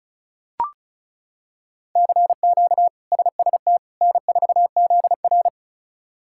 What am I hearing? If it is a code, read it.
Morse code, heard as a keyed tone, about 36 wpm, 700 Hz. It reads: CQ SST N4ZR
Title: K1USN SST QSO Practice - 35wpm